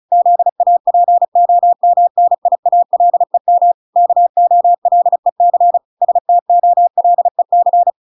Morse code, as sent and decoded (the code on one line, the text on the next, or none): --.. .- .--. --- -- -. .. .- .-.. -- / -.- --- .-.. -.-. / ... - --- .-.. -.-.
ZAPOMNIALEM KOLEC STOLEC